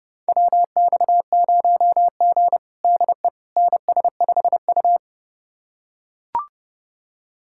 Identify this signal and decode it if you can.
Morse code, heard as a keyed tone, about 31 wpm, 700 Hz. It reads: WX0G DE NS5U
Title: USA Callsign DE Callsign - 4 Characters - 30wpm